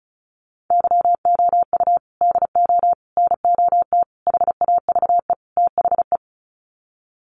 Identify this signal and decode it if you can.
Morse code, heard as a keyed tone, about 34 wpm, 700 Hz. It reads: YOU DO NOT HAVE THE
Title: Sets of 5 Words 35wpm